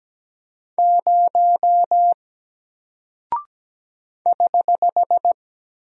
Morse code, heard as a keyed tone, about 17 wpm, 700 Hz. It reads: 0 <HH>
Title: Single Character (Letter:Number:Punctuation:Prosign) 17wpm